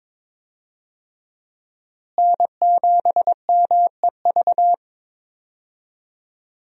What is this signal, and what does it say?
Morse code, heard as a keyed tone, about 22 wpm, 700 Hz. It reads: N7MEV
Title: USA Call Signs - 5 Characters - Repeated 2x - 22wpm